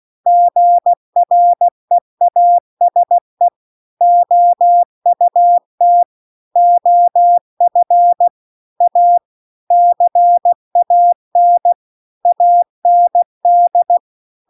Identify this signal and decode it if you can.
Morse code, heard as a keyed tone, about 16 wpm, 700 Hz. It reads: GREASE OUT OF A CAN AND
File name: CW-The-Big-Twoheaded-River-2-16WPM